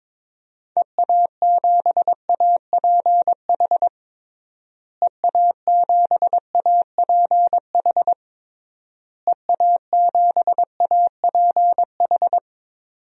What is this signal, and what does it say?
Morse code, heard as a keyed tone, about 22 wpm, 700 Hz. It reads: EA7APH EA7APH EA7APH
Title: DXCC-100 Call Signs - 6 Characters - Repeated 3x - 22wpm